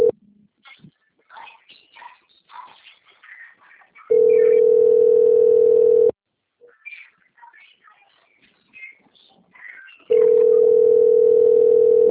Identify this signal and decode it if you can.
Ringback tone (North America), 440 + 480 Hz, with speech.
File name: out- - - - - .906